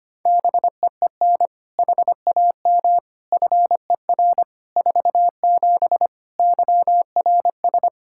Morse code, heard as a keyed tone, about 25 wpm, 700 Hz. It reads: BEEN HAM FER 47 YRS
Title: Practice QSO - Exchange 3 - More Info - 25wpm